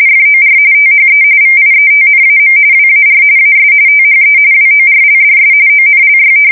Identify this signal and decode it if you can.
RTTY, about 45.45 baud. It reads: FOR THE VIETNAM WAR AS THE WORLDS ONLY
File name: fsk170 / battleshipnj